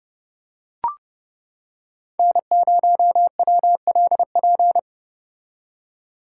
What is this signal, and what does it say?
Morse code, heard as a keyed tone, about 29 wpm, 700 Hz. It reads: N0WLP